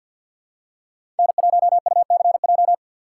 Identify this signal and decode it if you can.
Morse code, heard as a keyed tone, about 47 wpm, 700 Hz. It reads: N1UCJ
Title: USA Call Signs - 5 Characters - 4x Character Spacing - 50wpm